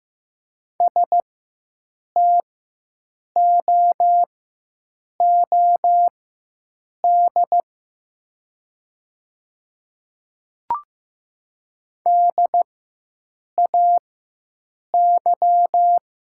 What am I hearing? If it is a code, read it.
Morse code, heard as a keyed tone, about 15 wpm, spaced as for 7 wpm, 700 Hz. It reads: STOOD DAY
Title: 4x Character Spacing - Top 500 Words - 15wpm